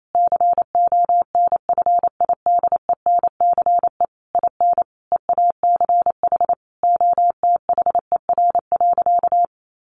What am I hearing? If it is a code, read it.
Morse code, heard as a keyed tone, about 28 wpm, 700 Hz. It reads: CONFIDENCE IN EACH OTHER.
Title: Sentences from Top 1500 Words 28wpm